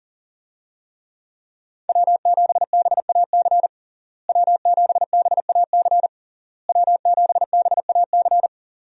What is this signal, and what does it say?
Morse code, heard as a keyed tone, about 40 wpm, 700 Hz. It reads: W7BAC W7BAC W7BAC
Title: USA Call Signs - 5 Characters - Repeated 3x - 40wpm